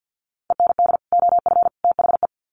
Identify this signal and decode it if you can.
Morse code, heard as a keyed tone, about 50 wpm, 700 Hz. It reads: END OF THE
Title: Sets of 3 Words 50wpm